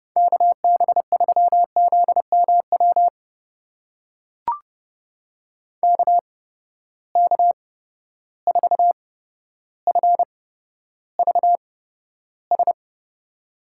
Morse code, heard as a keyed tone, about 30 wpm, spaced as for 13 wpm, 700 Hz. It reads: KB3ZMW KK4FVS
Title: USA Call Signs - 6 Characters - 8x Character Spacing - 30wpm